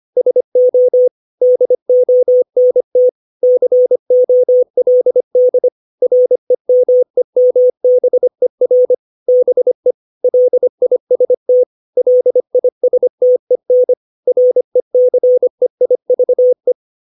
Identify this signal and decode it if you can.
Morse code, heard as a keyed tone, about 25 wpm, 500 Hz. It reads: SO DONT COLD REMEMBER BE LIST LISTEN RECEIVE
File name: 632 words / 632 Words 25WPM